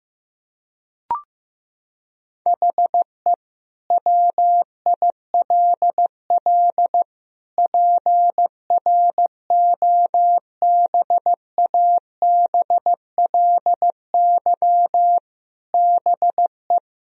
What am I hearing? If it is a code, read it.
Morse code, heard as a keyed tone, about 16 wpm, 700 Hz. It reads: HE WILL PROBABLY BE
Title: Sentences from Top 1000 Words - Review Part 4 - 15wpm